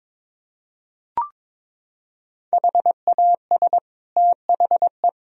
Morse code, heard as a keyed tone, about 22 wpm, 700 Hz. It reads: HAS THE
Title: Sentences from Top 1000 Words - Review Part 3 - 22wpm